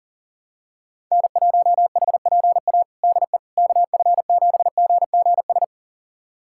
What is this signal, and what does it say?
Morse code, heard as a keyed tone, about 40 wpm, 700 Hz. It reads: N1HPA DE KF7GGS
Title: USA Callsign DE Callsign 40wpm